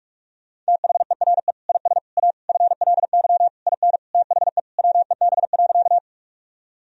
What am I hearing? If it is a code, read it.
Morse code, heard as a keyed tone, about 46 wpm, 700 Hz. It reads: THERE IS A FLY IN THE WEB.
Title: Sentences from Top 1000 Words - Review Part 4 - 45wpm